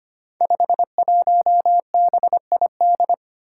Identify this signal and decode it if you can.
Morse code, heard as a keyed tone, about 25 wpm, 700 Hz. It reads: 51BID
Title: Five Challenging Letter-Number Combinations 25wpm